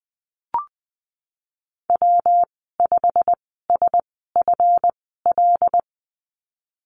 Morse code, heard as a keyed tone, about 20 wpm, spaced as for 14 wpm, 700 Hz. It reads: W5SFL
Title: 161 Lesson 26g - Letter g - Call Signs 20wpm